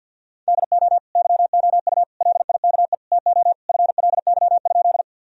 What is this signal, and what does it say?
Morse code, heard as a keyed tone, about 50 wpm, 700 Hz. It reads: DO YOU LIKE TO FLY?
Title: Sentences from Top 1000 Words - Review Part 2 - 50wpm